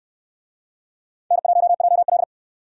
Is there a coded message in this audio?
Yes: Morse code from 1.3 to 2.2 s, heard as a keyed tone, about 65 wpm, 700 Hz. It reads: N1JL